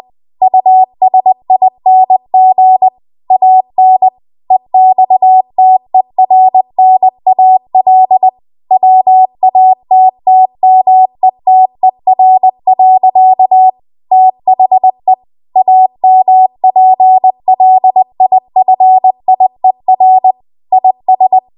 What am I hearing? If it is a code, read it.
Morse code, heard as a keyed tone, about 20 wpm, 750 Hz. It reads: USING AN EXTERNAL WATTMETER. THE AMPLIFIER IS
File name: Archive / 20 WPM / 20WPM